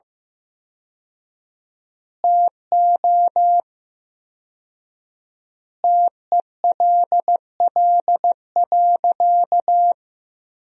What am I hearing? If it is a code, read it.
Morse code, heard as a keyed tone, about 16 wpm, 700 Hz. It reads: TO TELL.
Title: 4x Word-Spacing - Sentences from Top 200 Words - 15wpm